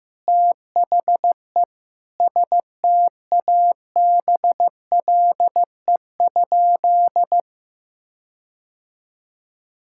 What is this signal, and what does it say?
Morse code, heard as a keyed tone, about 15 wpm, 700 Hz. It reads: THE STABLE?
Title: Sentences from Top 1600 Words 15wpm